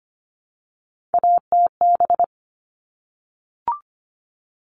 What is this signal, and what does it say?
Morse code, heard as a keyed tone, about 25 wpm, 700 Hz. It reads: ATB
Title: All Cut Numbers - 3-Digit 25wpm